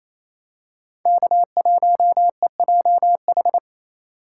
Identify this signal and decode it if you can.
Morse code, heard as a keyed tone, about 28 wpm, 700 Hz. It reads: K1EJH
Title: USA Call Signs - 4x Character Spacing - 28wpm